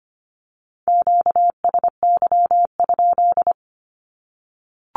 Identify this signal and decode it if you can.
Morse code, heard as a keyed tone, about 25 wpm, 700 Hz. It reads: QSY?
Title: Q-Codes Repeated 3x 25wpm